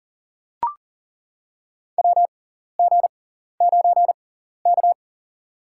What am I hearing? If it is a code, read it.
Morse code, heard as a keyed tone, about 40 wpm, spaced as for 14 wpm, 700 Hz. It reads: WG9K
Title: USA Call Signs - 6x Character Spacing - 40wpm